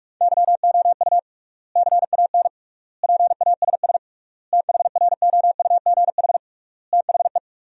Morse code, heard as a keyed tone, about 42 wpm, 700 Hz. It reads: YOU CAN PASS THROUGH THE
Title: 3x Word-Spacing - Sentences from Top 400 Words - 45wpm